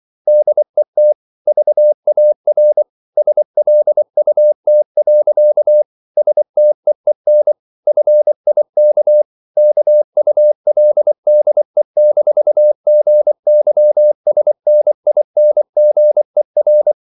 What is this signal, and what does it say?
Morse code, heard as a keyed tone, about 24 wpm, 600 Hz. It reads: DET VAR SLUT. STEEN FIK KULDE-GYSNINGER
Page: Ferie 24 wpm